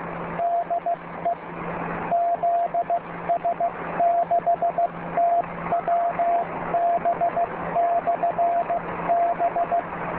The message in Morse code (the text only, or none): DE ZS6TWB/B